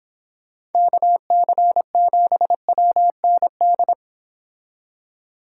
Morse code, heard as a keyed tone, about 29 wpm, 700 Hz. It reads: KC7WND